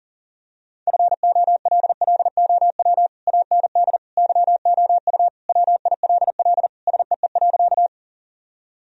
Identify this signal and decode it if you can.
Morse code, heard as a keyed tone, about 40 wpm, 700 Hz. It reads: FOLLOW AND YOU WILL SEE.
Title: Sentences from Top 1000 Words - Review Part 1 - 40wpm